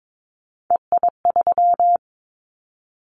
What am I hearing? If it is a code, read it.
Morse code, heard as a keyed tone, about 22 wpm, 700 Hz. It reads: EI3
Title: Callsign Prefixes for Top 50 DXCC - 22wpm